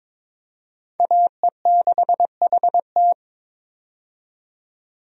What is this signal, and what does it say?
Morse code, heard as a keyed tone, about 22 wpm, 700 Hz. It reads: AE6HT
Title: USA Call Signs - 5 Characters - 22wpm